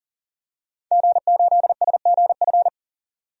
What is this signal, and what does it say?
Morse code, heard as a keyed tone, about 40 wpm, 700 Hz. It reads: G8SGF
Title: DXCC-100 Call Signs - 5 Characters - 40wpm